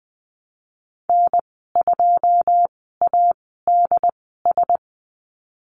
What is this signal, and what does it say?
Morse code, heard as a keyed tone, about 20 wpm, spaced as for 15 wpm, 700 Hz. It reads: N2ADS
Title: 147 Lesson 24g - Letter y - Call Signs 20wpm